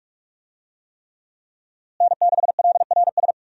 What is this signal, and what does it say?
Morse code, heard as a keyed tone, about 41 wpm, 700 Hz. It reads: N6LRS
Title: USA Call Signs - 5 Characters - Repeated 3x - 45wpm